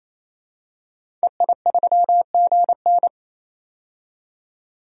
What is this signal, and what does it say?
Morse code, heard as a keyed tone, about 28 wpm, 700 Hz. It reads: EI3GN
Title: DXCC-100 Call Signs - 5 Characters - 28wpm